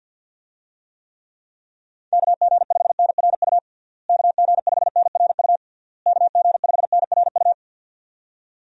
Morse code, heard as a keyed tone, about 50 wpm, 700 Hz. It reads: KG5NRU KG5NRU KG5NRU
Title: USA Call Signs - 6 Characters - Repeated 3x - 50wpm